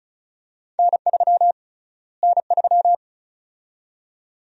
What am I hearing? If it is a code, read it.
Morse code, heard as a keyed tone, about 36 wpm, 700 Hz. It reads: N3 N3
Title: USA Call Sign Prefixes - Repeated 2x - 35wpm